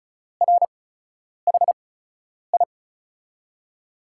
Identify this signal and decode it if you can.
Morse code, heard as a keyed tone, about 36 wpm, spaced as for 10 wpm, 700 Hz. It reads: RHI
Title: USA Call Signs - 5 Characters - 8x Character Spacing - 35wpm